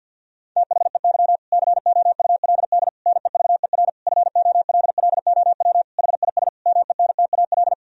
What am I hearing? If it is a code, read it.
Morse code, heard as a keyed tone, about 50 wpm, 700 Hz. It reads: THEY COULD NEVER FOLLOW HIS MENTAL
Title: Sentences from Top 1000 Words - Review Part 4 - 50wpm